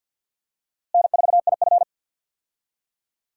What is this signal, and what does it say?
Morse code, heard as a keyed tone, about 52 wpm, 700 Hz. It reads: N4IF
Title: USA Call Signs - 4 Characters - Repeated 2x - 50wpm